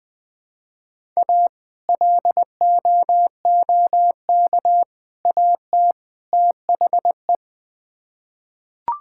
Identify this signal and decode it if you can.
Morse code, heard as a keyed tone, about 21 wpm, 700 Hz. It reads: A LOOK AT THE
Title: Build A Phrase - 4-Word Phrases - 20wpm